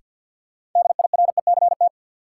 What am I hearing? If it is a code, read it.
Morse code, heard as a keyed tone, about 49 wpm, 700 Hz. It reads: DIRECT
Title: Sentences from Top 400 Words - New Word Drill - 50wpm